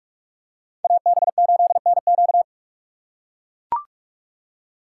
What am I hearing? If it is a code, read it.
Morse code, heard as a keyed tone, about 44 wpm, 700 Hz. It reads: AB8NQ